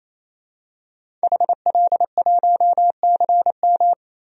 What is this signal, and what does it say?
Morse code, heard as a keyed tone, about 28 wpm, 700 Hz. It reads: HL1CM
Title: DXCC-100 Call Signs - 5 Characters - 28wpm